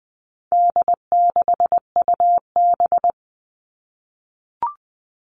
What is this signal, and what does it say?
Morse code, heard as a keyed tone, about 20 wpm, 700 Hz. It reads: D6UB
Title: All Cut Numbers - 4-Digit 20wpm